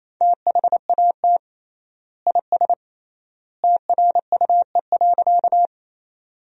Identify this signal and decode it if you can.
Morse code, heard as a keyed tone, about 28 wpm, 700 Hz. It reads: THAT IS TRUE.
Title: 3x Word-Spacing - Sentences from Top 400 Words - 28wpm